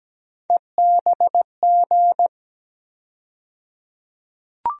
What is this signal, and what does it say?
Morse code, heard as a keyed tone, about 17 wpm, 700 Hz. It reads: EBG
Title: USA Call Sign Postfixes - 3 Letters 17wpm